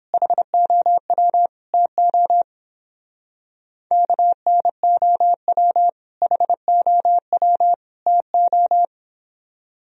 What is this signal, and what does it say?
Morse code, heard as a keyed tone, about 31 wpm, 700 Hz. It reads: HOW TO KNOW HOW TO